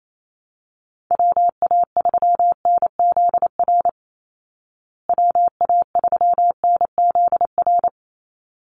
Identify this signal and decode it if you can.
Morse code, heard as a keyed tone, about 28 wpm, 690 Hz. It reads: WA3NZR WA3NZR
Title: Call Signs Repeated 3x - 28wpm